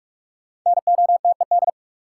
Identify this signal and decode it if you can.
Morse code, heard as a keyed tone, about 47 wpm, 700 Hz. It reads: NOTED
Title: Sets of 5 Words - New Word Drill - 45wpm